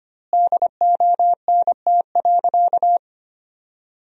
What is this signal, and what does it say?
Morse code, heard as a keyed tone, about 25 wpm, 700 Hz. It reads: DONT.
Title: 4x Word-Spacing - Sentences from Top 300 Words - 25wpm